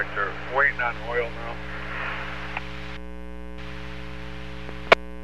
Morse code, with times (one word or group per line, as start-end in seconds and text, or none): none